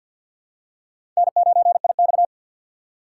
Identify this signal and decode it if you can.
Morse code, heard as a keyed tone, about 50 wpm, 700 Hz. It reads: N9IX
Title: USA Call Signs - 4 Characters - 3x Character Spacing - 50wpm